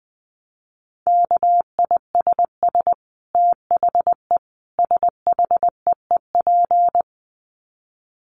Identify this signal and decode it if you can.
Morse code, heard as a keyed tone, about 20 wpm, 700 Hz. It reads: KISS THE SHEEP
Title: Dit Challenge 20wpm